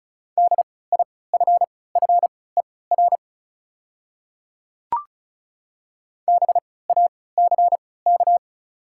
Morse code, heard as a keyed tone, about 34 wpm, spaced as for 20 wpm, 690 Hz. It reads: DIFFER BACK
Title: 3x Character Spacing - Top 200 Words - 35wpm